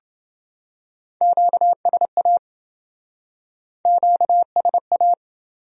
Morse code, heard as a keyed tone, about 27 wpm, 700 Hz. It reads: QSA QSA